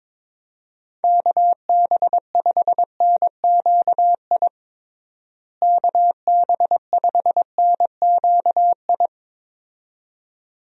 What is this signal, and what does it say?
Morse code, heard as a keyed tone, about 22 wpm, 700 Hz. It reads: KB5NQI KB5NQI
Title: USA Call Signs - 6 Characters - Repeated 2x - 22wpm